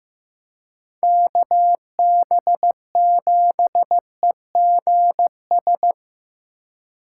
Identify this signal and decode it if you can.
Morse code, heard as a keyed tone, about 16 wpm, 700 Hz. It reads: KB7EGS
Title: USA Call Signs - 6 Characters - 4x Character Spacing - 15wpm